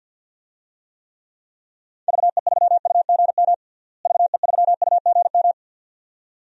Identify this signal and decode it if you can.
Morse code, heard as a keyed tone, about 47 wpm, 700 Hz. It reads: VE3UGM VE3UGM
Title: DXCC-100 Call Signs - 6 Characters - Repeated 2x - 50wpm